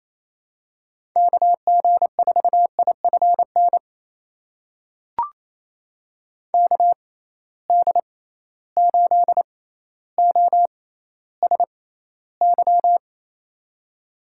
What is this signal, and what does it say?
Morse code, heard as a keyed tone, about 28 wpm, spaced as for 14 wpm, 700 Hz. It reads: KG4IFN KD8OSY
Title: USA Call Signs - 6 Characters - 6x Character Spacing - 28wpm